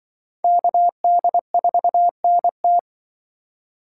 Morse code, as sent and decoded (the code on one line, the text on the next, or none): -.- -.. ....- -. -
KD4NT